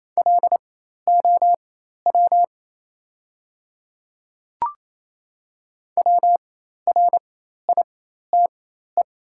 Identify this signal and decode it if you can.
Morse code, heard as a keyed tone, about 28 wpm, spaced as for 13 wpm, 700 Hz. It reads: LOW WRITE
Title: 4x Character Spacing - Top 200 Words - 28wpm